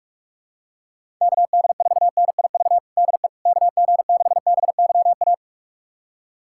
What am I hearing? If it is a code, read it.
Morse code, heard as a keyed tone, about 46 wpm, 700 Hz. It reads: KD4NIV DE KG6BYA
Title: USA Callsign DE Callsign 45wpm